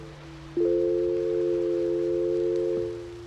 Telephone call progress tone (North America), heard as a telephone network's tone, about 350, 440 Hz, with noise. Dial tone.